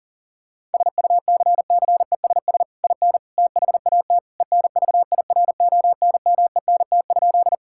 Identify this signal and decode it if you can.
Morse code, heard as a keyed tone, about 40 wpm, 700 Hz. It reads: SUCCESS IN THAT ENVIRONMENT?
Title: Sentences from Top 1000 Words - Review Part 5 - 40wpm